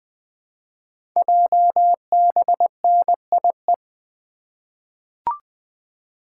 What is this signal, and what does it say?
Morse code, heard as a keyed tone, about 20 wpm, 700 Hz. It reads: JBNIE
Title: Code Groups - 5 Letters - 20wpm